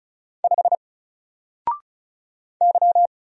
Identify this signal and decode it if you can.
Morse code, heard as a keyed tone, about 35 wpm, 700 Hz. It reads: 5 Y
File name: Single-Letter-Number-35wpm